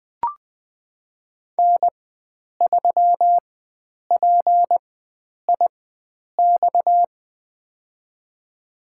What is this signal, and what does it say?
Morse code, heard as a keyed tone, about 20 wpm, spaced as for 9 wpm, 700 Hz. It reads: N3PIX
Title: USA Call Signs - 5 Characters - 4x Character Spacing - 20wpm